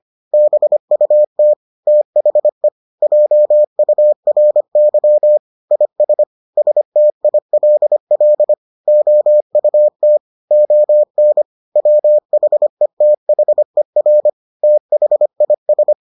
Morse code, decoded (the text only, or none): BUT THE JURY IS STILL OUT ON WHETHER THIS